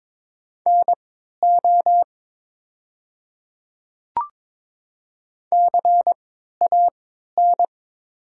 Morse code, heard as a keyed tone, about 22 wpm, spaced as for 12 wpm, 700 Hz. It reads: NO CAN
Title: 3x Character Spacing - Top 100 Words - 22wpm